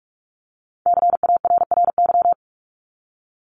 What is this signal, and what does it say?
Morse code, heard as a keyed tone, about 44 wpm, 700 Hz. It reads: CARRY
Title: Top 300 Words 45wpm